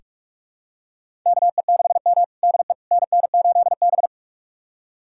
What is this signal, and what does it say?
Morse code, heard as a keyed tone, about 46 wpm, 700 Hz. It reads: KE6M DE NN8B
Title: USA Callsign DE Callsign - 4 Characters - 45wpm